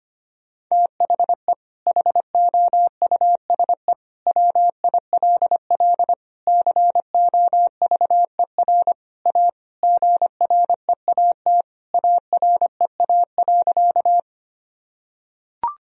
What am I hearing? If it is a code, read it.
Morse code, heard as a keyed tone, about 25 wpm, 700 Hz. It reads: THE HOUSE WILL COVER A GREAT AREA.
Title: Sentences from Top 1000 Words - Review Part 2 - 25wpm